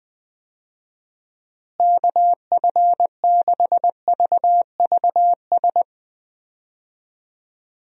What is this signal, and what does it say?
Morse code, heard as a keyed tone, about 20 wpm, 700 Hz. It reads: KF6VVS
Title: USA Call Signs - 6 Characters - Repeated 3x - 20wpm